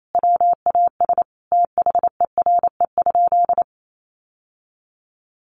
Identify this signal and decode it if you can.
Morse code, heard as a keyed tone, about 28 wpm, 700 Hz. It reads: WAS THERE?
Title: Sentences from Top 1200 Words 28wpm